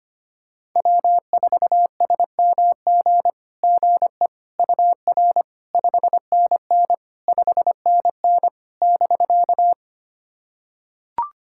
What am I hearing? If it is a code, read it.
Morse code, heard as a keyed tone, about 25 wpm, 700 Hz. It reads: W4SMG GE UR 5NN 5NN <BK>
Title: SOTA QSO Practice 25wpm